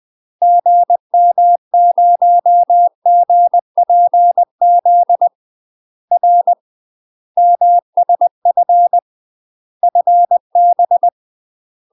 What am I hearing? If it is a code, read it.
Morse code, heard as a keyed tone, about 20 wpm, 700 Hz. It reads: GM0GPZ R MSF FB
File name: UK MASTER QSOs TRAINING / 20WPM / UK MASTER QSOs 0019